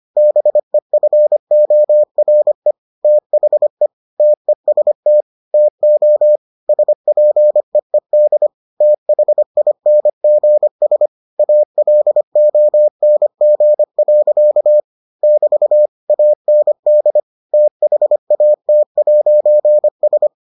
Morse code, heard as a keyed tone, about 25 wpm, 600 Hz. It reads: BEFORE THE TEST TO SPEED THINGS ALONG. <BT> AND THAT'S